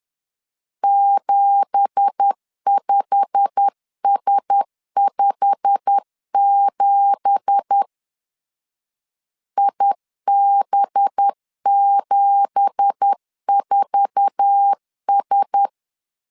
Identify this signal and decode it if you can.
Morse code, heard as a keyed tone, about 10 wpm, 800 Hz. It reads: 75S57 IB74S
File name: LEZIONI / 9 2867 10wpm 2733